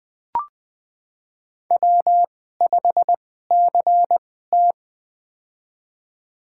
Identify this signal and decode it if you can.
Morse code, heard as a keyed tone, about 21 wpm, spaced as for 15 wpm, 700 Hz. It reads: W5CT
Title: USA Call Signs - 4 Characters - 2x Character Spacing - 20wpm